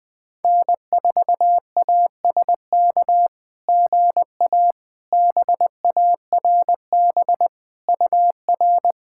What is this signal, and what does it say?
Morse code, heard as a keyed tone, about 20 wpm, 700 Hz. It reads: N4ASK GA BARB UR
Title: SOTA QSO Practice 20wpm